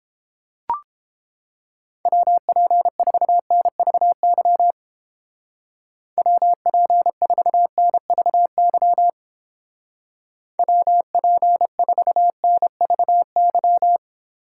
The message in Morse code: WP4NVY WP4NVY WP4NVY

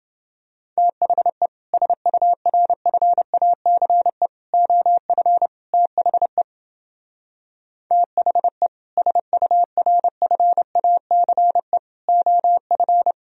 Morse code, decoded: THE SURFACE OF THE THE SURFACE OF

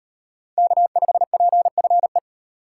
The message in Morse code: K5PFE

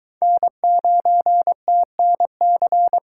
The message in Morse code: N9TNC

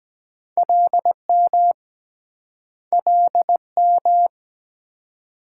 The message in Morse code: LM LM

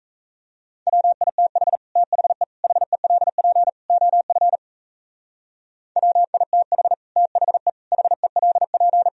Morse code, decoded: WITH THE HELP OF WITH THE HELP

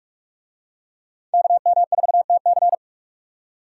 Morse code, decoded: KM4TC

